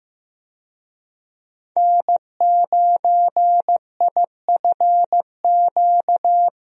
Morse code, heard as a keyed tone, about 15 wpm, 700 Hz. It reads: N9IFQ
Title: USA Call Signs - 5 Characters - 15wpm